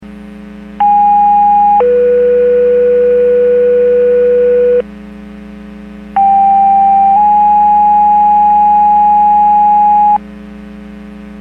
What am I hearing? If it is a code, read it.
Two-tone page, heard as a one-second tone, then a three-second tone, about 820 then 490 Hz, about 790 then 830 Hz. It is heard from 0.8 to 10.2 s.
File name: Portage 154.385 Paging